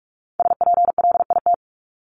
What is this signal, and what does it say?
Morse code, heard as a keyed tone, about 44 wpm, 700 Hz. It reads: SPLIT
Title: Top 1751-2000 Words 45wpm